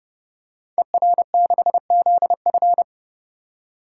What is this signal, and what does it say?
Morse code, heard as a keyed tone, about 30 wpm, 700 Hz. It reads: ER6ZF
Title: Code Groups - 5 Letter:Numbers - 30wpm